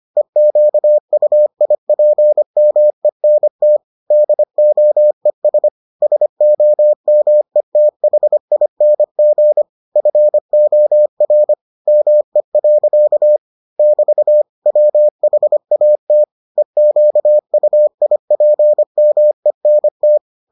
Morse code, heard as a keyed tone, about 25 wpm, 600 Hz. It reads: EQUIPMENT DOES SOMETHING FOR ME. <BT> WHAT EQUIPMENT